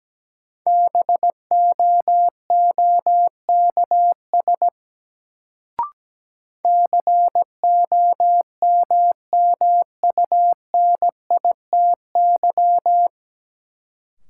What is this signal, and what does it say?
Morse code, heard as a keyed tone, about 17 wpm, 700 Hz. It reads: BOOKS COMMUNITY
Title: Sentences from Top 1000 Words - Review Part 4 - New Word Drill - 17wpm